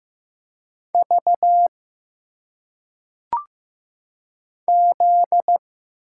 Morse code, heard as a keyed tone, about 15 wpm, 700 Hz. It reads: V Z
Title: Single Letters - Phonetic Pronunciation 15wpm